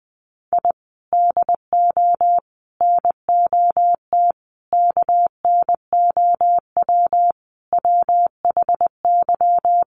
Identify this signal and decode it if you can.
Morse code, heard as a keyed tone, about 20 wpm, 700 Hz. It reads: I DO NOT KNOW WHY
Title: Sets of 5 Words 20wpm